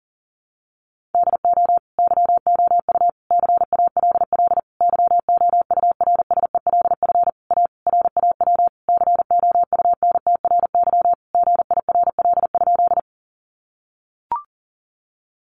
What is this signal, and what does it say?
Morse code, heard as a keyed tone, about 40 wpm, 700 Hz. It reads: DO YOU CALL YOURSELF A RAW COUNTRY GIRL?
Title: Sentences from Top 1500 Words 40wpm